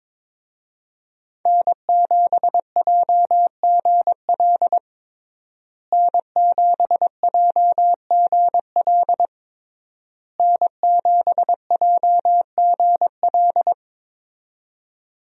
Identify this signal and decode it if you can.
Morse code, heard as a keyed tone, about 22 wpm, 700 Hz. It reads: N7JGL N7JGL N7JGL
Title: USA Call Signs - 5 Characters - Repeated 3x - 22wpm